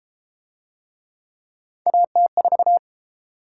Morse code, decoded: AT4

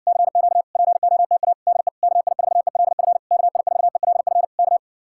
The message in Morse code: CQ POTA DE KI4ELF KI4ELF K

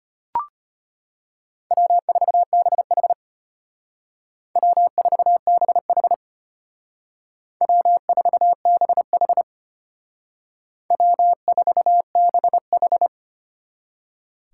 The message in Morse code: W4BH W4BH W4BH W4BH

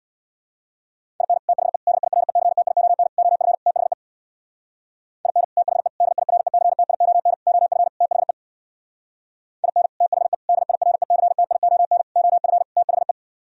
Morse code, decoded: IN THE DIRECTION OF THE IN THE DIRECTION OF THE IN THE DIRECTION OF THE

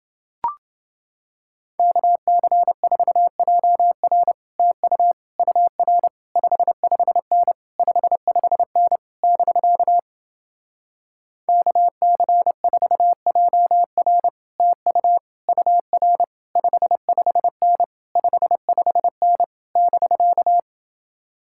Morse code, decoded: KC4JR TU UR 55N 55N <BK> KC4JR TU UR 55N 55N <BK>

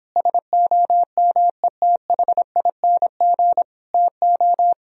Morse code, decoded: SOMETHING TO

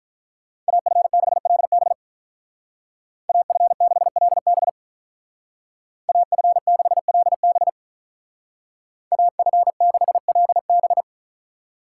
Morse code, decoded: AF6LB AF6LB AF6LB AF6LB

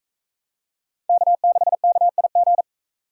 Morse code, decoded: K6KIG

